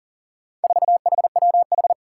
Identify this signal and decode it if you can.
Morse code, heard as a keyed tone, about 39 wpm, 700 Hz. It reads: 4HWH
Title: Four Challenging Letter-Number Combinations 40wpm